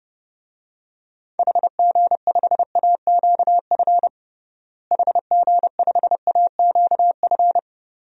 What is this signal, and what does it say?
Morse code, heard as a keyed tone, about 32 wpm, 700 Hz. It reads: HG5AQF HG5AQF
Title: DXCC-100 Call Signs - 6 Characters - Repeated 3x - 30wpm